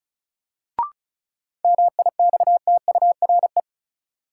Morse code, heard as a keyed tone, about 36 wpm, 700 Hz. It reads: MIXTURE
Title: Sentences from Top 1500 Words - New Word Drill Alt - 35wpm